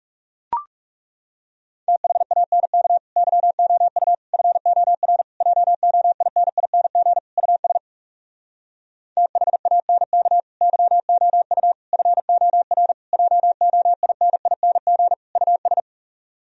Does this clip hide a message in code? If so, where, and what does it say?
Yes: Morse code from 1.9 to 15.8 s, heard as a keyed tone, about 40 wpm, 700 Hz. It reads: THANK YOU FOR JOINING US THANK YOU FOR JOINING US